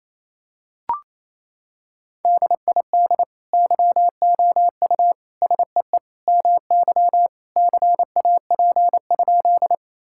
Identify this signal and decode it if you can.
Morse code, heard as a keyed tone, about 28 wpm, 700 Hz. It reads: DID YOU SEE MY CAP?
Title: Sentences from Top 1600 Words 28wpm